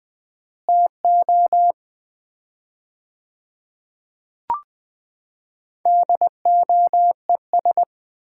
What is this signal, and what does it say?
Morse code, heard as a keyed tone, about 21 wpm, 700 Hz. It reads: TO DOES
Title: 4x Word-Spacing - Sets of 5 Words - 20wpm